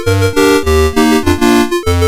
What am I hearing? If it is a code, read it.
There is no code.